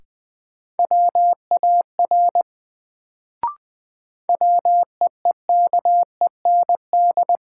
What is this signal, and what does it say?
Morse code, heard as a keyed tone, about 20 wpm, 700 Hz. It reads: WAR WEEKEND
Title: Sets of 5 Words - New Word Drill - 20wpm